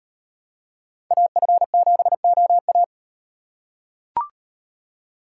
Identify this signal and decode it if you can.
Morse code, heard as a keyed tone, about 37 wpm, 700 Hz. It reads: AF7OA